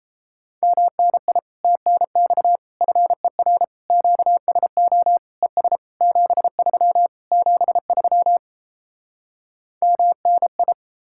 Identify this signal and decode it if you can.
Morse code, heard as a keyed tone, about 37 wpm, 700 Hz. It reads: MNI TNX FER QSO ES 73 73 MNI